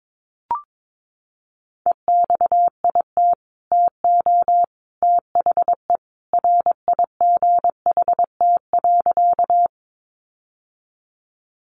Morse code, heard as a keyed tone, about 22 wpm, 700 Hz. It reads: EXIT TO THE RIGHT.
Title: Sentences from Top 1100 Words 22wpm